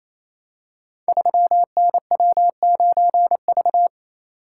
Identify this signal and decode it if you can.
Morse code, heard as a keyed tone, about 28 wpm, 700 Hz. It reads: 3NW9V
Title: Code Groups - 5 Letter:Numbers - 28wpm